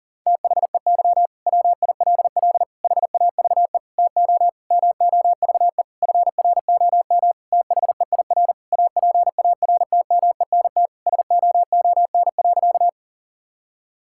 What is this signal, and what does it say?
Morse code, heard as a keyed tone, about 40 wpm, 700 Hz. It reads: THEY WILL HAVE TO MOVE FROM THEIR APARTMENT SOON.
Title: Sentences from Top 1000 Words - Review Part 6 - 40wpm